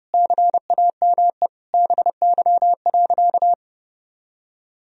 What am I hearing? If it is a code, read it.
Morse code, heard as a keyed tone, about 32 wpm, 700 Hz. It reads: CAME BY.
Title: Sentences from Top 1000 Words - Review Part 1 - 30wpm